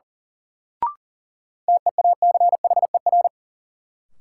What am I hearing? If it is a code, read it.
Morse code, heard as a keyed tone, about 37 wpm, 700 Hz. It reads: TEACHER
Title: Sentences from Top 900 Words - New Word Drill - 40wpm